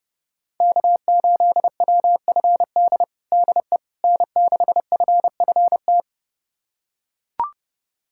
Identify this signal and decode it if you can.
Morse code, heard as a keyed tone, about 30 wpm, 700 Hz. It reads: K8WFD DE N6FFT
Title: USA Callsign DE Callsign - 5 Characters - 30wpm